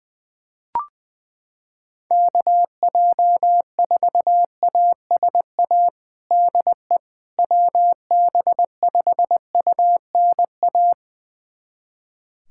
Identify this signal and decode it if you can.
Morse code, heard as a keyed tone, about 21 wpm, 700 Hz. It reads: KJ4ASA DE WB5UNA
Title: USA Callsign DE Callsign - 6 Characters - 20wpm